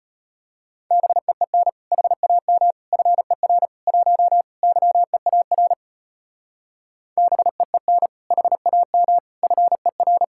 Morse code, heard as a keyed tone, about 31 wpm, 700 Hz. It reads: BEEN HAM FER 1 YEAR BEEN HAM FER
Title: Speed-Racing Common QSO Phrases - 25wpm